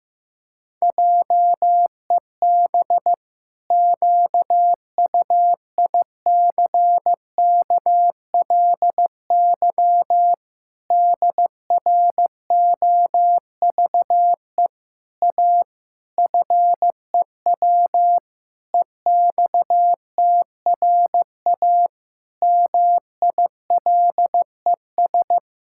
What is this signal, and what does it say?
Morse code, heard as a keyed tone, about 15 wpm, 700 Hz. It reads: JEB QUICKLY DROVE A FEW EXTRA MILES
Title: Pangram Challenge v2 - 15wpm